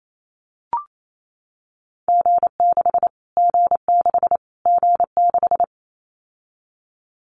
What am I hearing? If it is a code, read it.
Morse code, heard as a keyed tone, about 28 wpm, 700 Hz. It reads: G6 G6 G6
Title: Callsign Prefixes for Top 25 DXCC - Repeated 3x - 28wpm